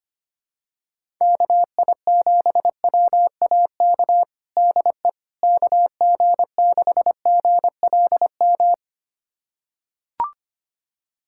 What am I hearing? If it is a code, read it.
Morse code, heard as a keyed tone, about 25 wpm, 700 Hz. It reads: KI7WAK DE KG6GLM
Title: USA Callsign DE Callsign - 6 Characters - 25wpm